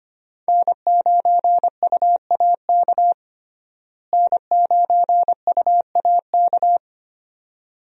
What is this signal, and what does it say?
Morse code, heard as a keyed tone, about 25 wpm, 700 Hz. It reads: N9UAK N9UAK
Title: USA Call Signs - 5 Characters - Repeated 2x - 25wpm